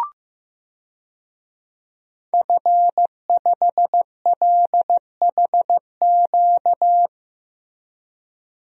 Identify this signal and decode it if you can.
Morse code, heard as a keyed tone, about 15 wpm, 700 Hz. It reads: F5LHQ
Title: DXCC-100 Call Signs - 5 Characters - Repeated 3x - 15wpm